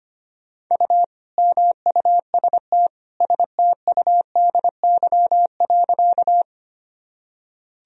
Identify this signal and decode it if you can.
Morse code, heard as a keyed tone, about 25 wpm, 700 Hz. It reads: U MUST STUDY.
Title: Sentences from Top 300 Words and Abbreviations 25wpm